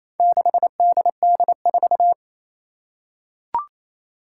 Morse code, heard as a keyed tone, about 29 wpm, 700 Hz. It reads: 6DD4
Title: Four Challenging Letter-Number Combinations 28wpm